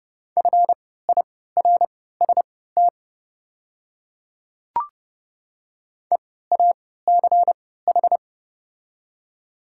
Morse code, heard as a keyed tone, about 30 wpm, spaced as for 17 wpm, 700 Hz. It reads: FIRST EACH
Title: 3x Character Spacing - Top 300 Words - 30wpm